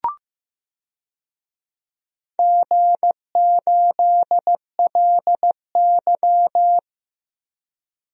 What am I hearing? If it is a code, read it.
Morse code, heard as a keyed tone, about 15 wpm, 700 Hz. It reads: G8LY
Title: DXCC-100 Call Signs - 4 Characters - Repeated 2x - 15wpm